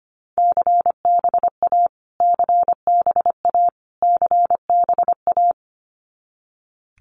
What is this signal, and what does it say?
Morse code, heard as a keyed tone, about 25 wpm, 700 Hz. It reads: CBA CBA CBA
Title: Abbreviations Repeated 3x - 25wpm